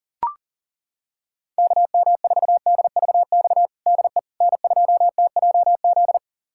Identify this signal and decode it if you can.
Morse code, heard as a keyed tone, about 40 wpm, 700 Hz. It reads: KM4DVX DE N2TJZ
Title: USA Callsign DE Callsign 40wpm